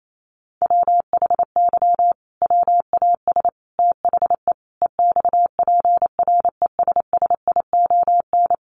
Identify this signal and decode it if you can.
Morse code, heard as a keyed tone, about 28 wpm, 700 Hz. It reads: WHY WAS THE EXPRESSION
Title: Sentences from Top 1400 Words 28wpm